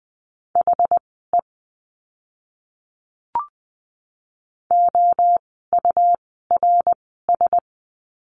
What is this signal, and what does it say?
Morse code, heard as a keyed tone, about 20 wpm, spaced as for 14 wpm, 700 Hz. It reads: HE OURS
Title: 101 Lesson 17f - Letter u - Words 20wpm